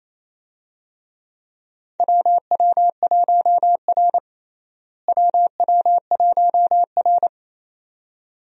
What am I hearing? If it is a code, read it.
Morse code, heard as a keyed tone, about 28 wpm, 700 Hz. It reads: WW1R WW1R
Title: USA Call Signs - 4 Characters - Repeated 2x - 28wpm